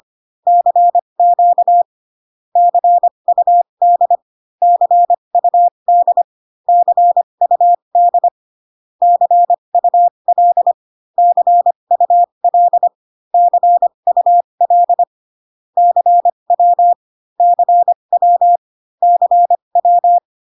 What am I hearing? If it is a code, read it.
Morse code, heard as a keyed tone, about 25 wpm, spaced as for 22 wpm, 700 Hz. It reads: CQ CUD CUD CUD CUL CUL CUL CW CW CW
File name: CW common abbreviations alphabetical repeated 3 times 25 18wpm